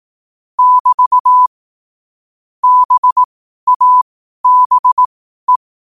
Morse code, heard as a keyed tone, about 18 wpm, spaced as for 12 wpm, 1000 Hz. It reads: <BT> BABE
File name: qotdfiles / 12wpm